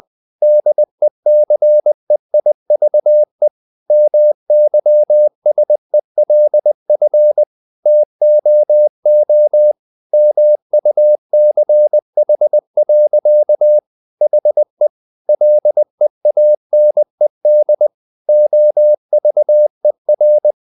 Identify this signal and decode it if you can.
Morse code, heard as a keyed tone, about 20 wpm, 600 Hz. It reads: DECEIVE MYSELF TOO MUCH. HE LEANED OVER
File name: the old man and the sea part 2 20wpm